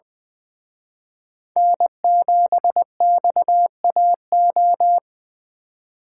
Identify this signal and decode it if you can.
Morse code, heard as a keyed tone, about 23 wpm, 700 Hz. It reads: N7XAO
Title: Speed Racing - USA 1x3 Call Signs 20wpm